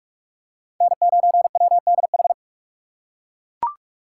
Morse code, heard as a keyed tone, about 38 wpm, 700 Hz. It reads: N9WDH